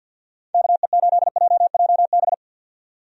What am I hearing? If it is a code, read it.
Morse code, heard as a keyed tone, about 51 wpm, 700 Hz. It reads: KE8JJB
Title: USA Call Signs - 6 Characters - 50wpm